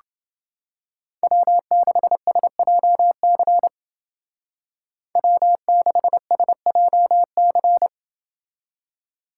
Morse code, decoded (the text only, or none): W6SJC W6SJC